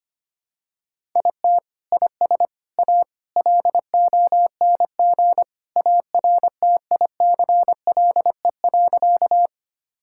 Morse code, heard as a keyed tone, about 25 wpm, 700 Hz. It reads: IT IS A LONG ARTICLE.
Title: Sentences from Top 1000 Words - Review Part 5 - 25wpm